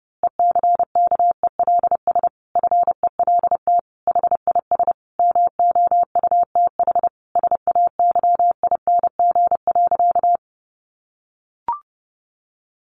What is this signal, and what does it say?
Morse code, heard as a keyed tone, about 30 wpm, 700 Hz. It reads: ECKELS FELT HIS MOUTH SAYING.
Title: A Sound Of Thunder 30wpm